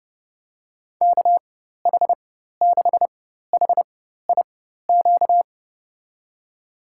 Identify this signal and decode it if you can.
Morse code, heard as a keyed tone, about 30 wpm, spaced as for 14 wpm, 700 Hz. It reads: KH6HIQ
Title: USA Call Signs - 6 Characters - 4x Character Spacing - 30wpm